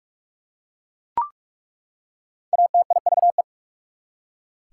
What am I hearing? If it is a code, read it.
Morse code, heard as a keyed tone, about 46 wpm, 690 Hz. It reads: ATIVE
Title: Prefixes and Suffixes 45wpm